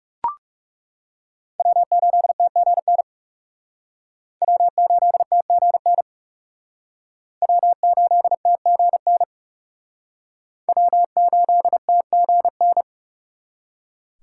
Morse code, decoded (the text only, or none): W8TGN W8TGN W8TGN W8TGN